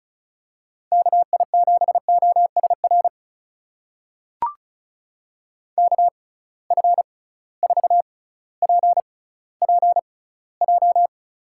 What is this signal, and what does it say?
Morse code, heard as a keyed tone, about 36 wpm, spaced as for 18 wpm, 700 Hz. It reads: KI7OSR KF4PPJ
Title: USA Call Signs - 6 Characters - 6x Character Spacing - 35wpm